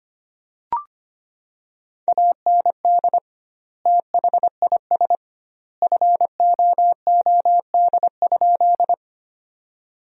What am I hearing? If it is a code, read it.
Morse code, heard as a keyed tone, about 25 wpm, 700 Hz. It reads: AND THIS FOOD?
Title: 2x Word-Spacing - Sentences from Top 300 Words - 25wpm